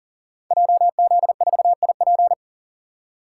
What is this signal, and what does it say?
Morse code, heard as a keyed tone, about 37 wpm, 700 Hz. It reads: JZ4IP
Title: Code Groups - 5 Letter:Numbers - 40wpm